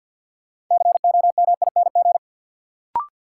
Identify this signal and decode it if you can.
Morse code, heard as a keyed tone, about 52 wpm, 700 Hz. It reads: COMING
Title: Sentences from Top 1300 Words - New Word Drill - 50wpm